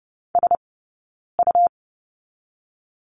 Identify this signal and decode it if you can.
Morse code, heard as a keyed tone, about 30 wpm, spaced as for 9 wpm, 690 Hz. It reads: SU
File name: Lesson 17d - Letter u - Two Characters 30wpm